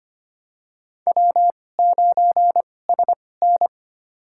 Morse code, heard as a keyed tone, about 25 wpm, spaced as for 18 wpm, 700 Hz. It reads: W9SN
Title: USA Call Signs - 4 Characters - 2x Character Spacing - 25wpm